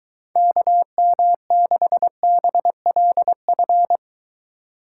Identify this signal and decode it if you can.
Morse code, heard as a keyed tone, about 23 wpm, 700 Hz. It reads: KM6BLF